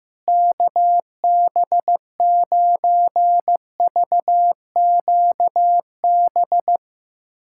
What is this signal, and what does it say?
Morse code, heard as a keyed tone, about 15 wpm, 700 Hz. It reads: KB9VQB
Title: USA Callsign DE Callsign 15wpm